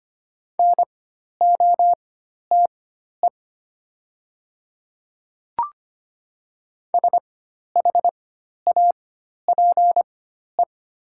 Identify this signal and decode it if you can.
Morse code, heard as a keyed tone, about 25 wpm, spaced as for 12 wpm, 700 Hz. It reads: NOTE SHAPE
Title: 4x Character Spacing - Top 500 Words - 25wpm